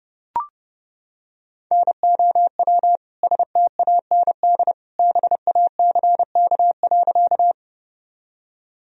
Morse code, heard as a keyed tone, about 30 wpm, 700 Hz. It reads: NOW STAND BACK.
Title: Sentences from Top 1000 Words - Review Part 1 - 30wpm